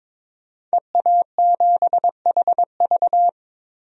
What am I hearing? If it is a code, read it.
Morse code, heard as a keyed tone, about 22 wpm, 700 Hz. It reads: EA7HV
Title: DXCC-100 Call Signs - 5 Characters - Repeated 2x - 22wpm